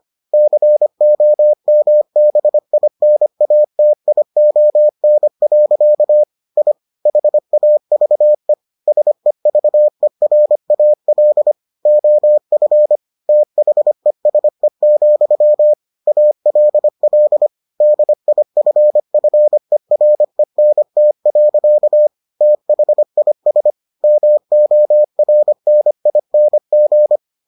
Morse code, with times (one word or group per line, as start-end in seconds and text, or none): 0.3-6.2 COMBINATION.
6.6-6.7 I
7.1-8.5 HAVE
8.9-11.5 SEVERAL
11.9-13.0 OF
13.3-15.7 THESE,
16.1-17.5 ALL
17.8-22.1 DIFFERENT.
22.4-23.7 THIS
24.0-27.2 MORNING